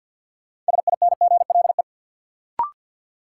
Morse code, heard as a keyed tone, about 53 wpm, 700 Hz. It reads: SINGLE
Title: Sentences from Top 700 Words - New Word Drill - 50wpm